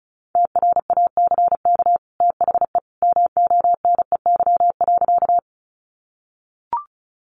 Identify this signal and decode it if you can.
Morse code, heard as a keyed tone, about 34 wpm, 700 Hz. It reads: TRACK THE MONEY.
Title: Sentences from Top 1100 Words 35wpm